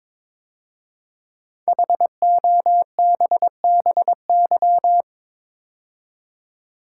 Morse code, heard as a keyed tone, about 22 wpm, 700 Hz. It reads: HOBBY
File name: 5 Letter Words - First 3 Letters Given - 22wpm